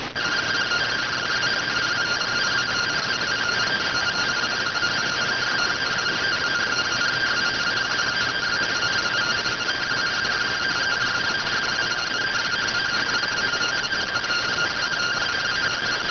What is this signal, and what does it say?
Two-tone teleprinter signal, about 170 Hz shift.